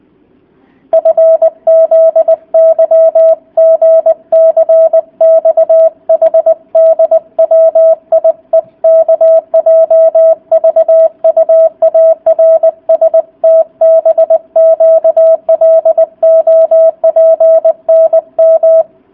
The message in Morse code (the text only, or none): FZYGCXHDWIEKJVUARSTBQLOPNM